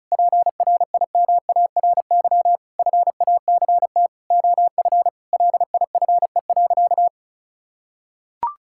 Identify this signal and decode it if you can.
Morse code, heard as a keyed tone, about 36 wpm, 700 Hz. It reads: PRIMARY FACT OF LIFE.
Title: Sentences from Top 1000 Words - Review Part 5 - 35wpm